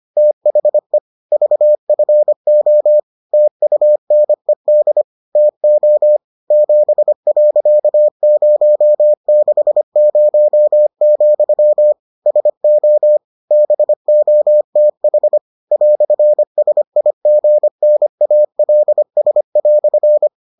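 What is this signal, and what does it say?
Morse code, heard as a keyed tone, about 25 wpm, 600 Hz. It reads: THE VFO TUNED TO 7.060, SO BOTH "SIGNALS"
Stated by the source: Morse edition of the Foundations of Amateur Radio episodes.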